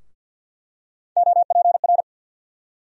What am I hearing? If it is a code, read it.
Morse code, heard as a keyed tone, about 47 wpm, 700 Hz. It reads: OPR
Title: USA Call Sign Postfixes - 3 Letters 50wpm